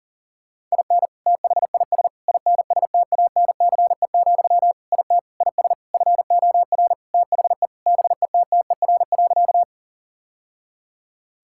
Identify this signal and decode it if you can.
Morse code, heard as a keyed tone, about 37 wpm, 700 Hz. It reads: IN THIS INSTANCE, IT IS FOR THE BETTER.
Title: Sentences from Top 1000 Words - Review Part 5 - 40wpm